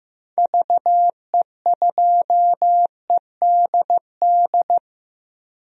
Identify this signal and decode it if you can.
Morse code, heard as a keyed tone, about 16 wpm, 700 Hz. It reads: VE2EDD
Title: DXCC-100 Call Signs - 6 Characters - Repeated 2x - 15wpm